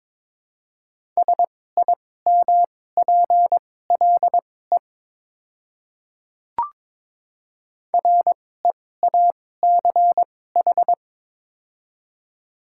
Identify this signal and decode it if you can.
Morse code, heard as a keyed tone, about 23 wpm, spaced as for 16 wpm, 690 Hz. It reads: SIMPLE REACH
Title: 2x Character Spacing - Top 500 Words - 22wpm